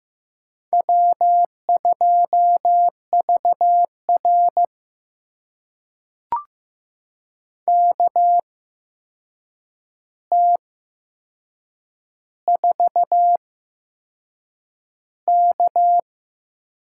Morse code, heard as a keyed tone, about 15 wpm, spaced as for 6 wpm, 700 Hz. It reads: W2VR KT4K
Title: USA Call Signs - 4 Characters - 8x Character Spacing - 15wpm